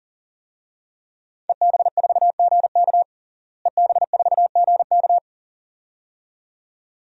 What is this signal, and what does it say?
Morse code, heard as a keyed tone, about 37 wpm, 700 Hz. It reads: EB4GK EB4GK
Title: DXCC-100 Call Signs - 5 Characters - Repeated 2x - 40wpm